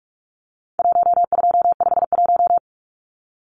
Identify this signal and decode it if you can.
Morse code, heard as a keyed tone, about 44 wpm, 700 Hz. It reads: 1251
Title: Numbers - Contest Style - 4-Digit 45wpm